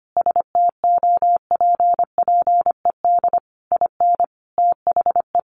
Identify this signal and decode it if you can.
Morse code, heard as a keyed tone, about 25 wpm, 700 Hz. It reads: STOPPED IN THE
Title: A Sound Of Thunder 25wpm